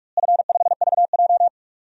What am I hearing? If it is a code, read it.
Morse code, heard as a keyed tone, about 44 wpm, 700 Hz. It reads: F5VJ
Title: DXCC-100 Call Signs - 4 Characters - Repeated 2x - 45wpm